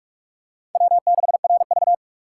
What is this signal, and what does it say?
Morse code, heard as a keyed tone, about 45 wpm, 700 Hz. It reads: W6RV
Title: USA Call Signs - 4 Characters - 4x Character Spacing - 45wpm